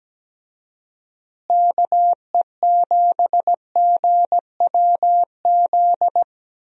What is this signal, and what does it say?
Morse code, heard as a keyed tone, about 17 wpm, 700 Hz. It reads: KE7GWZ
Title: USA Call Signs - 6 Characters - Repeated 3x - 17wpm